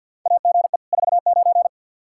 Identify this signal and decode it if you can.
Morse code, heard as a keyed tone, about 52 wpm, 700 Hz. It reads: AGE 49
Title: K1USN SST QSO Practice 50wpm